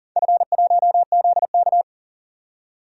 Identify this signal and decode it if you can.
Morse code, heard as a keyed tone, about 40 wpm, 690 Hz. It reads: F1ZK
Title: DXCC-100 Call Signs - 4 Characters - 40wpm